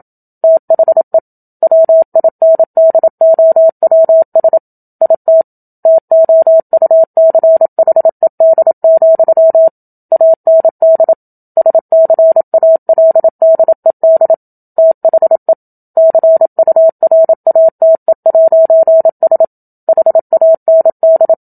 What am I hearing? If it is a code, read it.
Morse code, heard as a keyed tone, about 27 wpm, 650 Hz. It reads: THE WINDOWS IT TOUCHED, AND SCALDED THE CURATE'S HAND
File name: Book Two / 01-Chapter 1 - 27 WPM